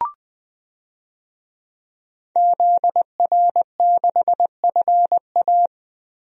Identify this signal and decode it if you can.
Morse code, heard as a keyed tone, about 20 wpm, 700 Hz. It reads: ZR6FA
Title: DXCC-100 Call Signs - 5 Characters - 20wpm